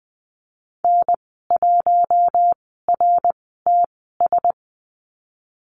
Morse code, heard as a keyed tone, about 20 wpm, spaced as for 15 wpm, 700 Hz. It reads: N1RTS
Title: 069 Lesson 12g - Letter d - Call Signs 20wpm